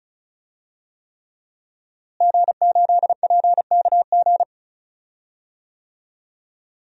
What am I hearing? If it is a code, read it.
Morse code, heard as a keyed tone, about 36 wpm, 700 Hz. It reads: G8PKG
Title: DXCC-100 Call Signs - 5 Characters - Repeated 3x - 35wpm